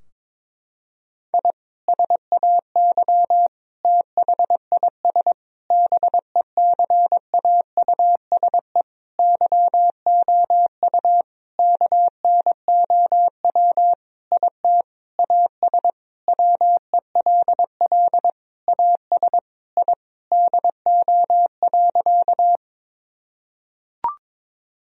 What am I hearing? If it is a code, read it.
Morse code, heard as a keyed tone, about 22 wpm, 700 Hz. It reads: I SAY THIS BECAUSE YOU KNOW IT AS WELL AS I DO.
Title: Sentences from Top 1000 Words - Review Part 3 - 22wpm